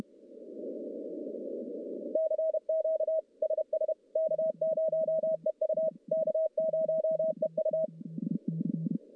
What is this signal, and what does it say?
Morse code, heard as a keyed tone, about 31 wpm, 640 Hz. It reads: CQ SS K0EU K0EU
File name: contests / CW-SS-k5tr- -1200